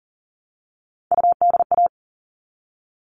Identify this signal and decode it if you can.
Morse code, heard as a keyed tone, about 40 wpm, 700 Hz. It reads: UDA
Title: All Cut Numbers - 3-Digit 40wpm